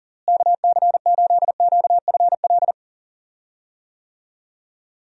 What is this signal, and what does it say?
Morse code, heard as a keyed tone, about 40 wpm, 700 Hz. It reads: KC8QFL
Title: USA Call Signs - 6 Characters - Repeated 3x - 40wpm